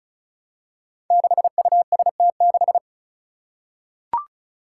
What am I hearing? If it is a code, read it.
Morse code, heard as a keyed tone, about 36 wpm, 700 Hz. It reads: 6UST6
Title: Five Challenging Letter-Number Combinations 35wpm